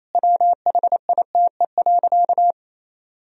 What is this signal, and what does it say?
Morse code, heard as a keyed tone, about 28 wpm, 700 Hz. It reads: WHITE.
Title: Sentences from Top 1000 Words - Review Part 2 - 28wpm